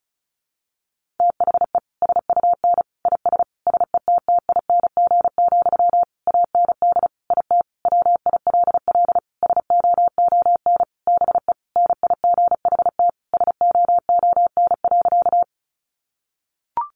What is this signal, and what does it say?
Morse code, heard as a keyed tone, about 34 wpm, 700 Hz. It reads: THE SUN IS SETTING, AND IT WILL SOON BE NIGHT SOON.
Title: Sentences from Top 1300 Words 35wpm